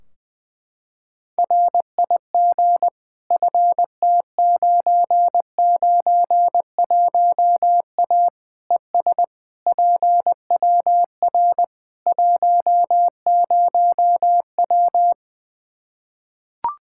Morse code, heard as a keyed tone, about 20 wpm, 700 Hz. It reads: RIG FT991A ES PWR 10W
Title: Practice QSO - Exchange 2 - Info - 20wpm